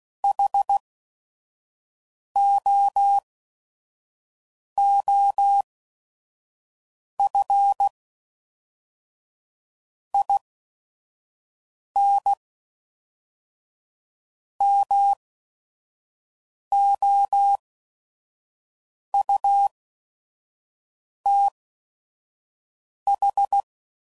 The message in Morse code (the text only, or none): HOOF IN MOUTH